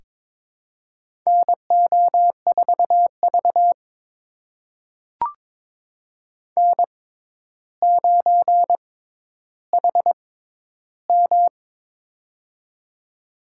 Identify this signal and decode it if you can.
Morse code, heard as a keyed tone, about 22 wpm, spaced as for 11 wpm, 700 Hz. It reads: NO4V N9HM
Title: USA Call Signs - 4 Characters - 6x Character Spacing - 22wpm